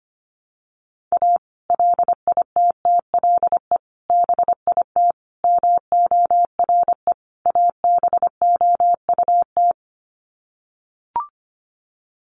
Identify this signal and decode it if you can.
Morse code, heard as a keyed tone, about 25 wpm, 700 Hz. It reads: A LITTLE BIT MORE ABOUT
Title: Sets of 5 Words Encore2 25wpm